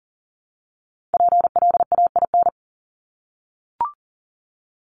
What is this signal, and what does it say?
Morse code, heard as a keyed tone, about 40 wpm, 700 Hz. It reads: PLAIN
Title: Top 500 Words 40wpm